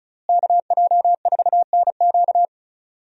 Morse code, heard as a keyed tone, about 34 wpm, 700 Hz. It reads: KJ4NQ
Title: USA Call Signs - 5 Characters - Repeated 3x - 35wpm